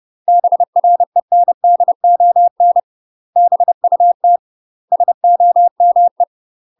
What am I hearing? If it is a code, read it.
Morse code, heard as a keyed tone, about 30 wpm, 700 Hz. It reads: BRENDON BUT SOME
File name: UK MASTER QSOs TRAINING / 30WPM / UK MASTER QSOs 0046